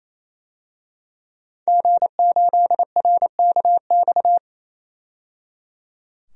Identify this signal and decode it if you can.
Morse code, heard as a keyed tone, about 27 wpm, 700 Hz. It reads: G8RKX
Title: DXCC-100 Call Signs - 5 Characters - 28wpm